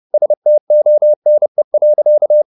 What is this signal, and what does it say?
Morse code, heard as a keyed tone, about 30 wpm, 600 Hz. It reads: STONE.